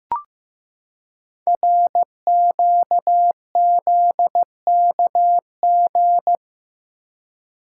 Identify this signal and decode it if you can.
Morse code, heard as a keyed tone, about 15 wpm, 700 Hz. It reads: RQZKG
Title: Code Groups - 5 Letter:Numbers - 15wpm